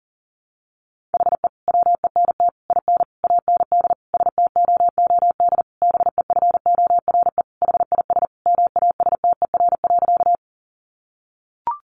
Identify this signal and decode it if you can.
Morse code, heard as a keyed tone, about 37 wpm, 700 Hz. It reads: HE WENT IN AND STOOD BEFORE HIS MASTER.
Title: Sentences from Top 1300 Words 40wpm